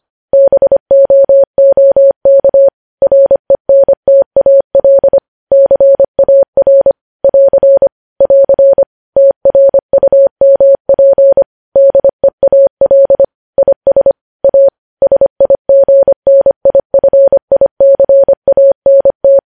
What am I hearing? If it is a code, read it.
Morse code, heard as a keyed tone, about 25 wpm, 550 Hz. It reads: BOOK RENTAL CAR <AR> <AR> TRUMP DEAL IS A SIGNIFICANT
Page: news headlines in morse code at 25 wpm